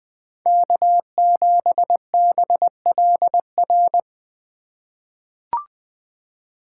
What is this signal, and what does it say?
Morse code, heard as a keyed tone, about 20 wpm, 700 Hz. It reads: K7BLR
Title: USA Call Signs - 5 Characters - 4x Character Spacing - 20wpm